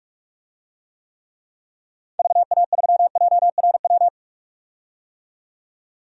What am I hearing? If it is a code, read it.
Morse code, heard as a keyed tone, about 45 wpm, 700 Hz. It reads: VA3JRW
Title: DXCC-100 Call Signs - 6 Characters - 45wpm